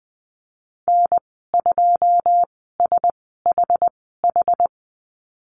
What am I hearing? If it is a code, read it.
Morse code, heard as a keyed tone, about 20 wpm, spaced as for 15 wpm, 700 Hz. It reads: N2SHH
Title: 088 Lesson 15g - Number 5 - Call Signs 20wpm